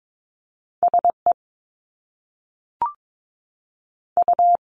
Morse code, heard as a keyed tone, about 22 wpm, 700 Hz. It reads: SE U
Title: Top Two Letter Combinations in Words 22wpm